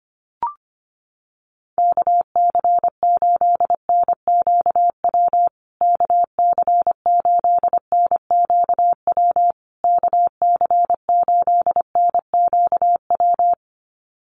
Morse code, heard as a keyed tone, about 25 wpm, 700 Hz. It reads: KC8NQW KC8NQW KC8NQW
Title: USA Call Signs Repeated 3x 25wpm